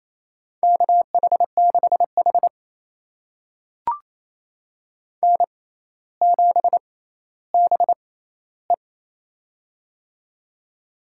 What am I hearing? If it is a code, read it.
Morse code, heard as a keyed tone, about 27 wpm, spaced as for 14 wpm, 700 Hz. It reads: KH6H N7BE
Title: USA Call Signs - 4 Characters - 6x Character Spacing - 28wpm